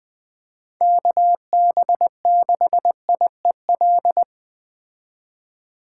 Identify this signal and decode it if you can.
Morse code, heard as a keyed tone, about 21 wpm, 700 Hz. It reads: KB6IEL
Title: Speed Racing - USA 2x3 Call Signs 17wpm